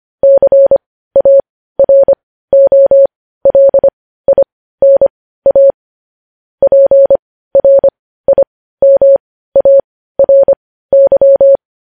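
Morse code, heard as a keyed tone, about 25 wpm, spaced as for 15 wpm, 550 Hz. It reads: CAROLINA PRIMARY
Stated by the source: Morse news headlines practice at 15.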